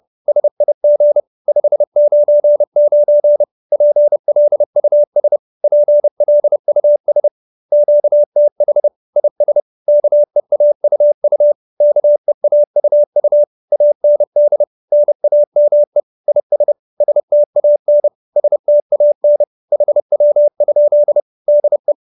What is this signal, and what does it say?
Morse code, heard as a keyed tone, about 30 wpm, 590 Hz. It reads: SIG 599 PLUS PLUS QTH IS KEAUU KEAUU AND NAME IS STAN STAN HW? DE